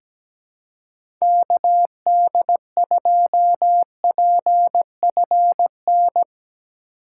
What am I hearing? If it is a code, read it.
Morse code, heard as a keyed tone, about 17 wpm, 700 Hz. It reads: KD2PFN
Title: USA Call Signs - 6 Characters - 4x Character Spacing - 17wpm